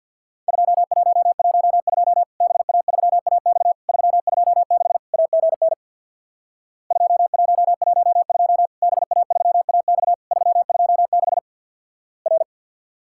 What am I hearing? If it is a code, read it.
Morse code, heard as a keyed tone, about 49 wpm, 700 Hz. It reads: 2112 BA3AX 326 AGN 2112 BA3AX 326 R
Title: Worked All Europe DX Contest - QTC Exchange with Repeats - 50wpm